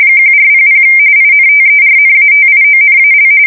RTTY, about 45.45 baud. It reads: 1400 INTERNATIONAL
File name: fsk170 / teletype-corp